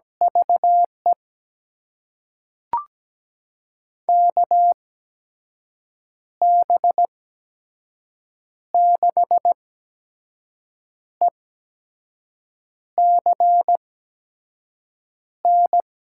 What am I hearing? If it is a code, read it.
Morse code, heard as a keyed tone, about 17 wpm, spaced as for 5 wpm, 700 Hz. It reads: VE KB6ECN
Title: USA Call Signs - 6 Characters - 8x Character Spacing - 17wpm